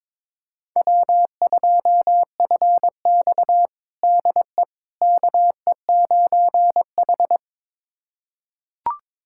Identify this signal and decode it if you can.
Morse code, heard as a keyed tone, about 22 wpm, 700 Hz. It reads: W2FX DE KE9H
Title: USA Callsign DE Callsign - 4 Characters - 22wpm